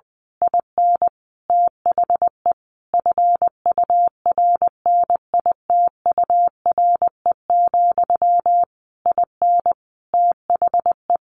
Morse code, heard as a keyed tone, about 20 wpm, 700 Hz. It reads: IN THE FURNITURE, IN THE
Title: A Sound Of Thunder 20wpm